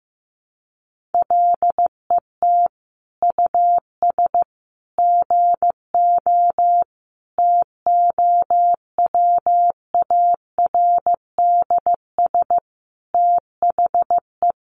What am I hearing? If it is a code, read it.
Morse code, heard as a keyed tone, about 15 wpm, 700 Hz. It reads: LET US GO TOWARDS THE
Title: Sentences from Top 1100 Words 15wpm